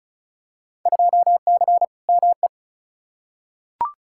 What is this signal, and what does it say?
Morse code, heard as a keyed tone, about 34 wpm, 700 Hz. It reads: 2C ME
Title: Field Day Exchanges Encore - 35wpm